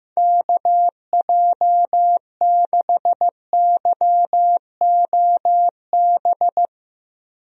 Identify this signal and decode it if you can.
Morse code, heard as a keyed tone, about 15 wpm, 700 Hz. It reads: KJ6YOB
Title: USA Call Signs - 6 Characters - Repeated 2x - 15wpm